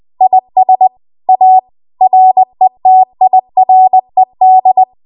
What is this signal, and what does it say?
Morse code, heard as a keyed tone, about 20 wpm, 750 Hz. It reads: IS A RETIRED
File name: Archive / 20 WPM / 20WPM